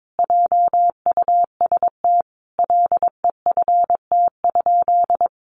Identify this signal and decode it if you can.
Morse code, heard as a keyed tone, about 22 wpm, 700 Hz. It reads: JUST LEFT?
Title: Sentences from Top 1200 Words 22wpm